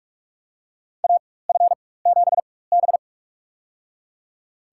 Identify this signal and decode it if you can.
Morse code, heard as a keyed tone, about 41 wpm, spaced as for 20 wpm, 700 Hz. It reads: AF7B
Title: USA Call Signs - 4x Character Spacing - 45wpm